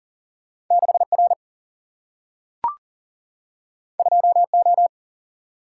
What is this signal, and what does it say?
Morse code, heard as a keyed tone, about 38 wpm, 700 Hz. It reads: 6 R 2 O
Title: Single-Letter Single-Number 40wpm